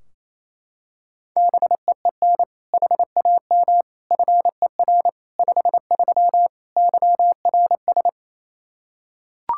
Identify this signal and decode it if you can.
Morse code, heard as a keyed tone, about 28 wpm, 700 Hz. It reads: BEEN HAM FER 53 YRS
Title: Practice QSO - Exchange 3 - More Info - 28wpm